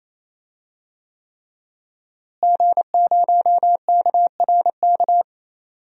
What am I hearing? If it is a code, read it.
Morse code, heard as a keyed tone, about 28 wpm, 700 Hz. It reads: G0KRK
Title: DXCC-100 Call Signs - 5 Characters - Repeated 3x - 28wpm